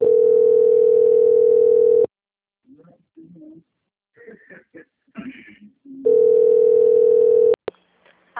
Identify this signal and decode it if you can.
Ringback tone (North America), 440 + 480 Hz, with speech.
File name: out- - - - - .46656